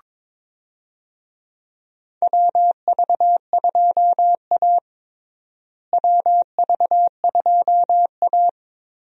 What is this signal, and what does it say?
Morse code, heard as a keyed tone, about 22 wpm, 700 Hz. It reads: WV2A WV2A
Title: USA Call Signs - 4 Characters - Repeated 2x - 22wpm